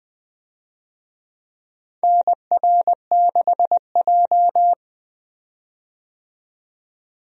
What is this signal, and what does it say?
Morse code, heard as a keyed tone, about 20 wpm, 700 Hz. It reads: NR6J
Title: USA Call Signs - 4 Characters - 20wpm